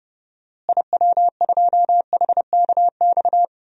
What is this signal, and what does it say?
Morse code, heard as a keyed tone, about 30 wpm, 700 Hz. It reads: IW2HKX
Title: DXCC-100 Call Signs - 6 Characters - Repeated 3x - 30wpm